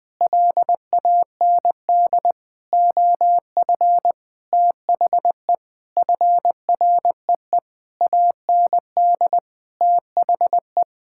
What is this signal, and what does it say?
Morse code, heard as a keyed tone, about 21 wpm, 700 Hz. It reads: LAND OF THE FREE AND THE
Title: Sentences with 9-Word Phrases - Spoken Beginning - 20wpm